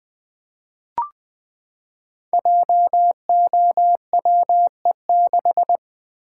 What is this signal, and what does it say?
Morse code, heard as a keyed tone, about 20 wpm, 700 Hz. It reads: JOWE6
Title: Code Groups - 5 Letter:Numbers - 20wpm